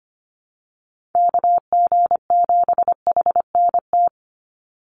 Morse code, heard as a keyed tone, about 25 wpm, 700 Hz. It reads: KG7HNT
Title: USA Call Signs 25wpm